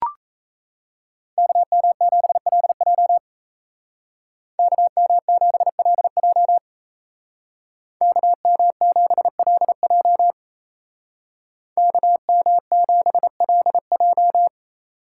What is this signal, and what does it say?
Morse code, heard as a keyed tone, about 36 wpm, 700 Hz. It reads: KM7LJ KM7LJ KM7LJ KM7LJ